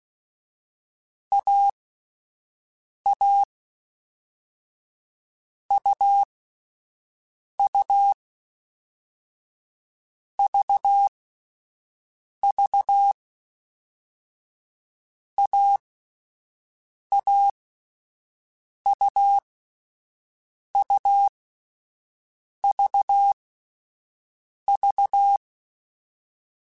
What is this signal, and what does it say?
Morse code, heard as a keyed tone, about 16 wpm, 800 Hz. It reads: A A U U V V A A U U V V